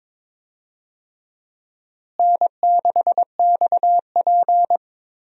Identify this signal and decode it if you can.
Morse code, heard as a keyed tone, about 22 wpm, 700 Hz. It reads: N6XP
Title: USA Call Signs - 4 Characters - Repeated 2x - 22wpm